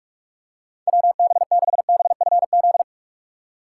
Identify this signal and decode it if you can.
Morse code, heard as a keyed tone, about 47 wpm, 700 Hz. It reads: WB6BFZ